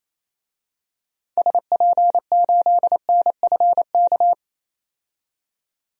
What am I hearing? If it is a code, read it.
Morse code, heard as a keyed tone, about 28 wpm, 700 Hz. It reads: SP8NFK
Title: DXCC-100 Call Signs - 6 Characters - 28wpm